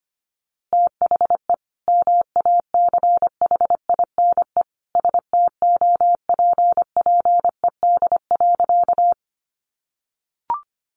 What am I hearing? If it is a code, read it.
Morse code, heard as a keyed tone, about 25 wpm, 700 Hz. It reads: THE MACHINE STOPPED.
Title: A Sound Of Thunder 25wpm